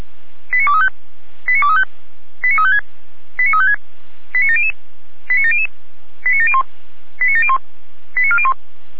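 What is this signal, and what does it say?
Five-tone selective call (ZVEI), heard as quick runs of five tones, tones of about 70 ms. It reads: 89235 89235 89356 89356 89700 89700 89701 89701 89401
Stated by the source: ZVEI five-tone selective call received by SDR, recorded on the Feuerwehr Orlamünde server.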